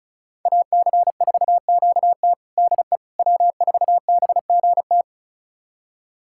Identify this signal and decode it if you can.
Morse code, heard as a keyed tone, about 36 wpm, 700 Hz. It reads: AC4QT DE W4BGT
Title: USA Callsign DE Callsign - 5 Characters - 35wpm